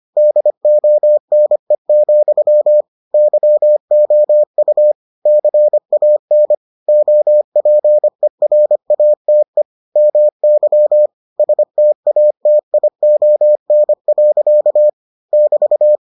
Morse code, heard as a keyed tone, about 25 wpm, 600 Hz. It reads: DONE, YOU CAN OPERATE MY STATION. <BT>